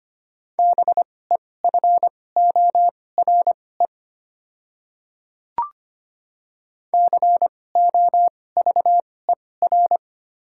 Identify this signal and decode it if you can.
Morse code, heard as a keyed tone, about 25 wpm, spaced as for 18 wpm, 690 Hz. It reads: BEFORE COVER
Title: 2x Character Spacing - Top 300 Words - 25wpm